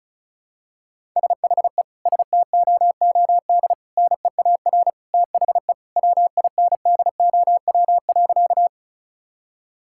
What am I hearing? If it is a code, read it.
Morse code, heard as a keyed tone, about 36 wpm, 700 Hz. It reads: SHE STOOD NEAR THE WINDOW.
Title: Sentences from Top 1600 Words 35wpm